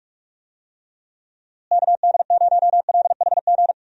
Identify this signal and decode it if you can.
Morse code, heard as a keyed tone, about 44 wpm, 700 Hz. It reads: KD0LHG
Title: USA Call Signs - 6 Characters - Repeated 2x - 45wpm